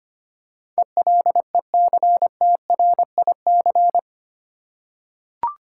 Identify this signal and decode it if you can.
Morse code, heard as a keyed tone, about 25 wpm, 690 Hz. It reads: ELECTRIC
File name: 8 Letter Words - First 3 Letters Given - 25wpm